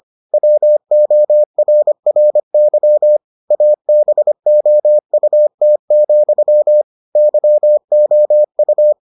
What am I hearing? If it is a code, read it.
Morse code, heard as a keyed tone, about 25 wpm, 600 Hz. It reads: WORRY ABOUT, YOU